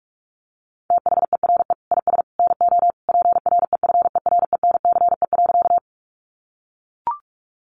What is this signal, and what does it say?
Morse code, heard as a keyed tone, about 44 wpm, 700 Hz. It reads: THERE IS NO PREFERENCE.
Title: Sentences from Top 1500 Words 45wpm